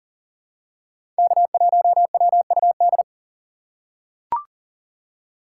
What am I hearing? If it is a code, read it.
Morse code, heard as a keyed tone, about 40 wpm, 700 Hz. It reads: K1WUD
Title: USA Call Signs - 5 Characters - 2x Character Spacing - 40wpm